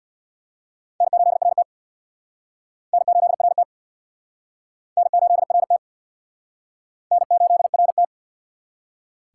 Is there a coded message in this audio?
Yes: Morse code from 1.0 to 8.0 s, heard as a keyed tone, about 64 wpm, 700 Hz. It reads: N8RT N8RT N8RT N8RT